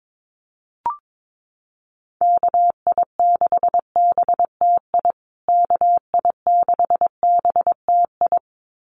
Morse code, heard as a keyed tone, about 22 wpm, 700 Hz. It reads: KI6BTI KI6BTI
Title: USA Call Signs Repeated 2x 22wpm